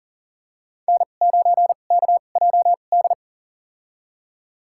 Morse code, heard as a keyed tone, about 40 wpm, spaced as for 29 wpm, 700 Hz. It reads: N9KJD
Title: USA Call Signs - 5 Characters - 2x Character Spacing - 40wpm